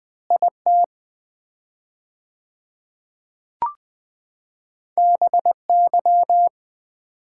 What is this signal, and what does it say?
Morse code, heard as a keyed tone, about 20 wpm, 700 Hz. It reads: IT BY
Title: 4x Word-Spacing - Sets of 4 Words - 20wpm